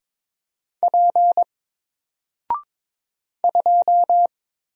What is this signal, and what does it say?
Morse code, heard as a keyed tone, about 22 wpm, 700 Hz. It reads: P 2
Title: Single Character (Letter:Number:Punctuation:Prosign) Alt 22wpm